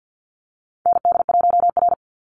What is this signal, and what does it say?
Morse code, heard as a keyed tone, about 48 wpm, 700 Hz. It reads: ND1R
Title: Call Signs Repeated 3x - 50wpm